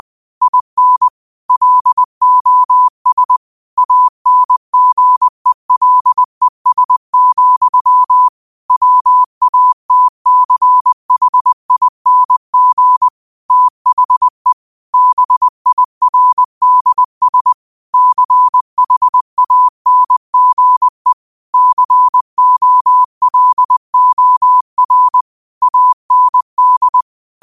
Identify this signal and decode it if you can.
Morse code, heard as a keyed tone, about 20 wpm, 1000 Hz. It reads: IN LOS ANGELES, WATCHING THE BIRDS CHANGE COLOR AND
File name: qotdfiles / 20wpm